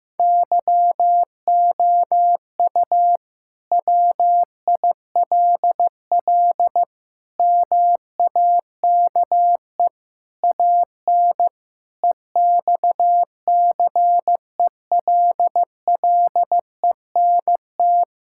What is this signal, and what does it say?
Morse code, heard as a keyed tone, about 16 wpm, 700 Hz. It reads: YOU WILL MAKE AN EXCELLENT
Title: Sentences from Top 1000 Words - Review Part 5 - 15wpm